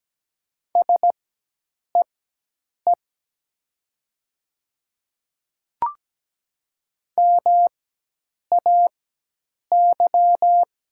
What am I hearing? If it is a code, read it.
Morse code, heard as a keyed tone, about 17 wpm, spaced as for 8 wpm, 700 Hz. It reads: SEE MAY
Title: 4x Character Spacing - Top 100 Words - 17wpm